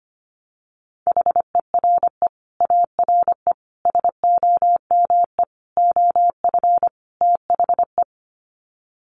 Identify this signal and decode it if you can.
Morse code, heard as a keyed tone, about 25 wpm, 700 Hz. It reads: HERE ARE SOME OF THE
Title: Sets of 5 Words Encore 25wpm